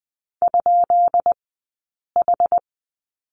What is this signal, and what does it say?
Morse code, heard as a keyed tone, about 20 wpm, 700 Hz. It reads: ? H
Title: 204 Lesson 33d - Letter j - Two Characters 20wpm